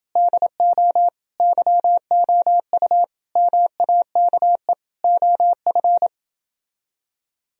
Morse code, heard as a keyed tone, about 28 wpm, 700 Hz. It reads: DO YOU MAKE OF